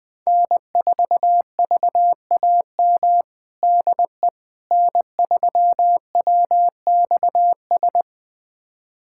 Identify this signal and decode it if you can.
Morse code, heard as a keyed tone, about 20 wpm, 700 Hz. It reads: N4VAM DE N3WXS
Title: USA Callsign DE Callsign - 5 Characters - 20wpm